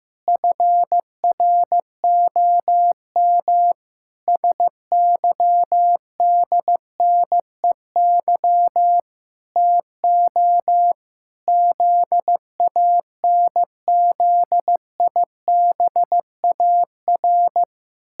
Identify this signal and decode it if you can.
Morse code, heard as a keyed tone, about 15 wpm, 700 Hz. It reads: FROM SYDNEY TO ZANZIBAR
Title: Pangram Challenge v2 - 15wpm